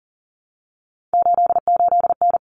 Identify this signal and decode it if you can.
Morse code, heard as a keyed tone, about 40 wpm, 700 Hz. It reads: 88N
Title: Common Cut Numbers - 3-Digit 40wpm